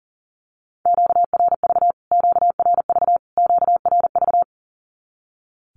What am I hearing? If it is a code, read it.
Morse code, heard as a keyed tone, about 40 wpm, 700 Hz. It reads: QRV QRV QRV
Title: Q-Codes Repeated 3x 40wpm